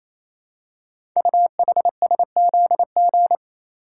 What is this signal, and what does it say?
Morse code, heard as a keyed tone, about 29 wpm, 700 Hz. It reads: UHSZG
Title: Code Groups - 5 Letters - 28wpm